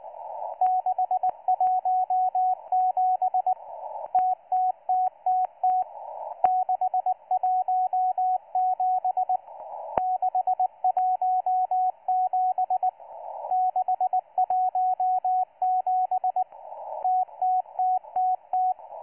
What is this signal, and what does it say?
Morse code, heard as a keyed tone, about 19 wpm, 740 Hz, with radio noise. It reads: 617 TTTTT 617 617 617 TTTTT